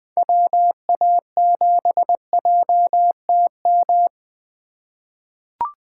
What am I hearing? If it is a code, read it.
Morse code, heard as a keyed tone, about 20 wpm, 700 Hz. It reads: WA7JTM
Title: 2023 SOTA Chaser Call Signs - 20wpm